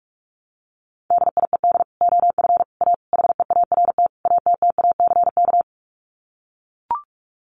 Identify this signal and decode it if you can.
Morse code, heard as a keyed tone, about 44 wpm, 700 Hz. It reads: DIED OF A HEART ATTACK
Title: Sets of 5 Words Encore2 45wpm